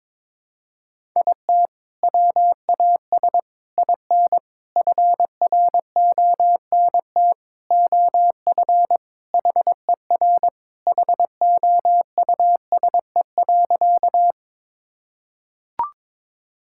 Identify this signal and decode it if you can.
Morse code, heard as a keyed tone, about 22 wpm, 700 Hz. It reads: IT WAS IN FRONT OF HER HOUSE.
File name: practice / 22wpm / Sentences with 4-Word Phrases - 22wpm